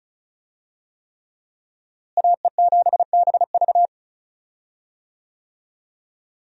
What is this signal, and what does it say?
Morse code, heard as a keyed tone, about 34 wpm, 700 Hz. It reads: AE7BV
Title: USA Call Signs - 5 Characters - Repeated 3x - 35wpm